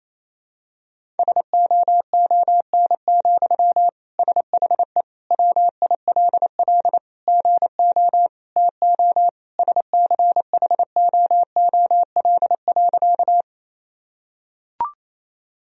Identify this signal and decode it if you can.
Morse code, heard as a keyed tone, about 28 wpm, 700 Hz. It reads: SOON, SHE WILL GO TO SCHOOL.
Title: Sentences from Top 1000 Words - Review Part 1 - 28wpm